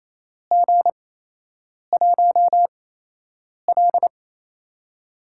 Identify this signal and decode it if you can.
Morse code, heard as a keyed tone, about 28 wpm, spaced as for 8 wpm, 700 Hz. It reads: G1L
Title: USA Call Signs - 4 Characters - 8x Character Spacing - 28wpm